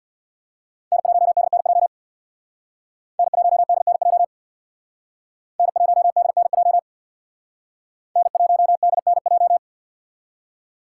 Morse code, heard as a keyed tone, about 65 wpm, 700 Hz. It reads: N1DNJ N1DNJ N1DNJ N1DNJ